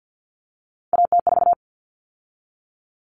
Morse code, heard as a keyed tone, about 50 wpm, 700 Hz. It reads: AT4
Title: Common Cut Numbers - 3-Digit 50wpm